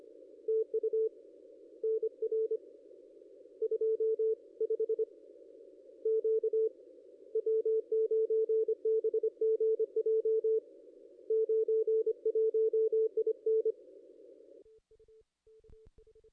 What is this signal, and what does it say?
Morse code, heard as a keyed tone, about 24 wpm, 440 Hz. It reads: TU NR 25 Q W9BGJ 91 IN
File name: contests / ss-cw-k5tr-op-n5rz -1400